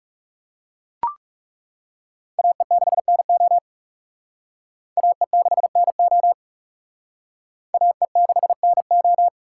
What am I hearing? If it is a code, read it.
Morse code, heard as a keyed tone, about 38 wpm, 700 Hz. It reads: AE6NO AE6NO AE6NO